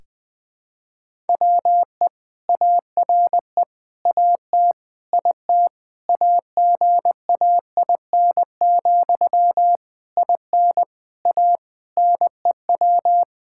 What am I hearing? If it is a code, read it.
Morse code, heard as a keyed tone, about 20 wpm, 700 Hz. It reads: WE ARE AT IT AGAIN, IN A NEW
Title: Sentences from Top 1000 Words - Review Part 5 - 20wpm